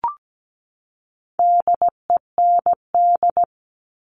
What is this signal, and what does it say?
Morse code, heard as a keyed tone, about 17 wpm, 700 Hz. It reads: DEND
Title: All Cut Numbers - 4-Digit 17wpm